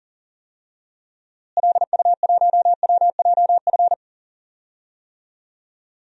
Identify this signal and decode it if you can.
Morse code, heard as a keyed tone, about 40 wpm, 700 Hz. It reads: LU1WJF
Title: DXCC-100 Call Signs - 6 Characters - Repeated 2x - 40wpm